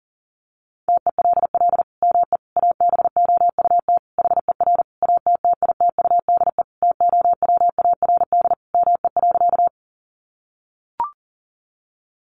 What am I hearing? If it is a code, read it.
Morse code, heard as a keyed tone, about 40 wpm, 700 Hz. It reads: TELL ME ABOUT HER ATTITUDE TOWARD ME.
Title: Sentences from Top 1300 Words 40wpm